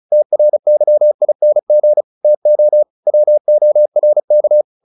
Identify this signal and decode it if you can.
Morse code, heard as a keyed tone, about 35 wpm, 600 Hz. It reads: TRYING TO WORK